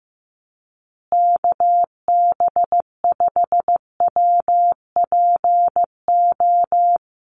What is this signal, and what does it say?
Morse code, heard as a keyed tone, about 15 wpm, 700 Hz. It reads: KB5WPO
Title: USA Call Signs Repeated 2x 15wpm